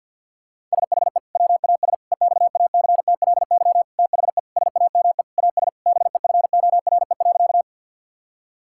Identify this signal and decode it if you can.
Morse code, heard as a keyed tone, about 50 wpm, 700 Hz. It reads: SHE WAS EXACTLY THE SAME AS BEFORE.
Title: Sentences from Top 1000 Words - Review Part 6 - 50wpm